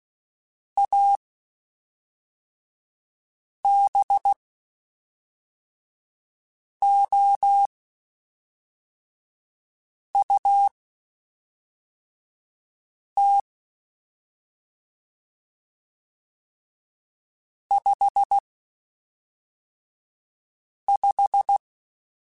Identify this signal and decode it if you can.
Morse code, heard as a keyed tone, about 16 wpm, spaced as for 3 wpm, 800 Hz. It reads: ABOUT 55
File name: Spaced Characters / Spaced Characters-mp3s / Weather4 24 sp @5